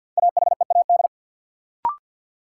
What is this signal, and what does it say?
Morse code, heard as a keyed tone, about 52 wpm, 700 Hz. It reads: AHEAD
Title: Sentences from Top 1100 Words - New Word Drill Alt - 50wpm